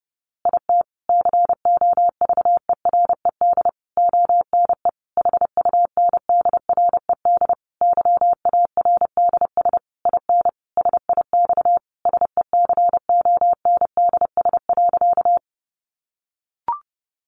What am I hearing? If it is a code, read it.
Morse code, heard as a keyed tone, about 30 wpm, 700 Hz. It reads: IT COVERED ONE HUNDRED YARDS IN SIX SECONDS.
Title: A Sound Of Thunder 30wpm